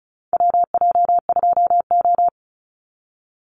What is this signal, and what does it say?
Morse code, heard as a keyed tone, about 36 wpm, 700 Hz. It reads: WJ2O
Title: Call Signs - 35wpm